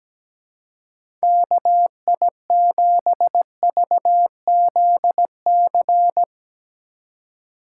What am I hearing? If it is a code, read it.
Morse code, heard as a keyed tone, about 17 wpm, 700 Hz. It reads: KI7VZC
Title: USA Call Signs - 6 Characters - 3x Character Spacing - 17wpm